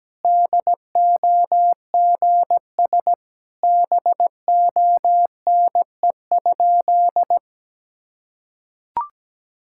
Morse code, heard as a keyed tone, about 17 wpm, 700 Hz. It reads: DOGS BONE?
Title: Sentences from Top 1600 Words 17wpm